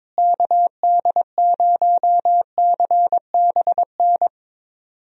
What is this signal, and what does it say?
Morse code, heard as a keyed tone, about 22 wpm, 700 Hz. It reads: KD0CBN
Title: USA Call Signs - 6 Characters - Repeated 3x - 22wpm